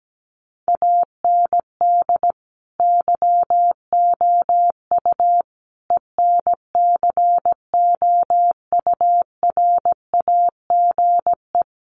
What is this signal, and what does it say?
Morse code, heard as a keyed tone, about 17 wpm, 700 Hz. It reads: AND YOU ENCOURAGE
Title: Sentences from Top 1400 Words 17wpm